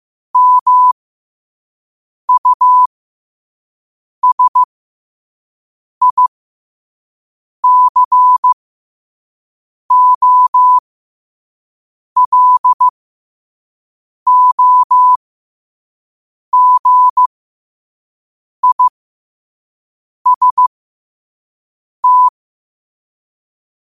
Morse code, heard as a keyed tone, about 15 wpm, spaced as for 5 wpm, 1000 Hz. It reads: MUSICOLOGIST
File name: qotdfiles / 5wpm